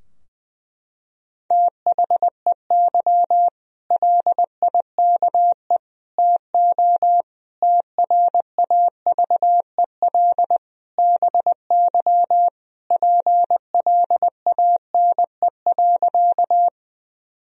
Morse code, heard as a keyed tone, about 20 wpm, 700 Hz. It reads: THEY LIKE TO TRAVEL BY PLANE.
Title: Sentences from Top 1000 Words - Review Part 3 - 20wpm